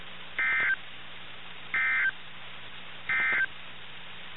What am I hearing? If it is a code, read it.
Emergency Alert System (SAME) tones, heard as harsh buzzing bursts. It is heard from 0.3 to 3.5 s.
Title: NOAA Weather Radio - KIH61